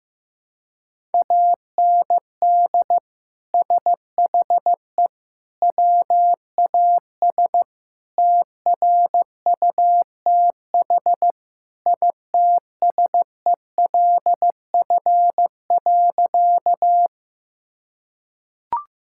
Morse code, heard as a keyed tone, about 15 wpm, 700 Hz. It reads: AND SHE WAS TRUTH ITSELF.
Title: Sentences from Top 1000 Words - Review Part 5 - 15wpm